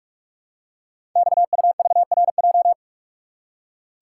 Morse code, heard as a keyed tone, about 44 wpm, 700 Hz. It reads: XUVRJ
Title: Code Groups - 5 Letters - 45wpm